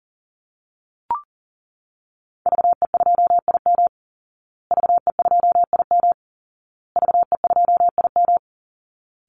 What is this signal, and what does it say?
Morse code, heard as a keyed tone, about 40 wpm, 700 Hz. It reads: VE2IM VE2IM VE2IM
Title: Call Signs Repeated 2x - 40wpm